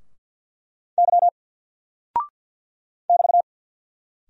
Morse code, heard as a keyed tone, about 54 wpm, 700 Hz. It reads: Y <BT>
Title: Single Character (Letter:Number:Punctuation:Prosign) 50wpm